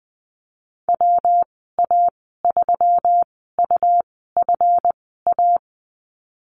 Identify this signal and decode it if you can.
Morse code, heard as a keyed tone, about 20 wpm, spaced as for 15 wpm, 700 Hz. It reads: WA3UFA
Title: 167 Lesson 27g - Number 7 - Call Signs 20wpm